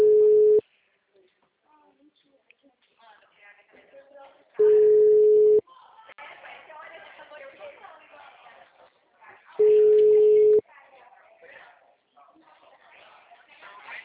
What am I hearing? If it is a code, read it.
Ringback tone (Europe, 425 Hz), with speech.